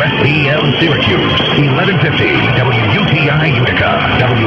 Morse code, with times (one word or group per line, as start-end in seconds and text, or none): none